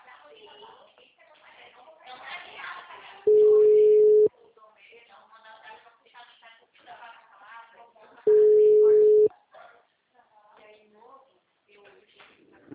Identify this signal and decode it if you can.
Ringback tone (Europe, 425 Hz), with speech.